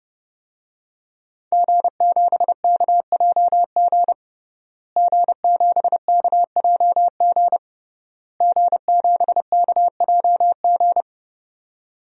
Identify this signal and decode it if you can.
Morse code, heard as a keyed tone, about 30 wpm, 700 Hz. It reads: G7KJG G7KJG G7KJG
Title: DXCC-100 Call Signs - 5 Characters - Repeated 3x - 30wpm